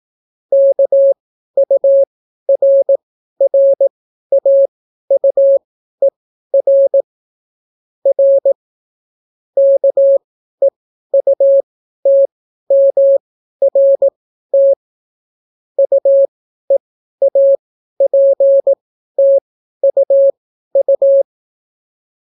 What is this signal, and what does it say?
Morse code, heard as a keyed tone, about 18 wpm, spaced as for 12 wpm, 550 Hz. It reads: KURRAUER R KEUTMRT UEAPTUU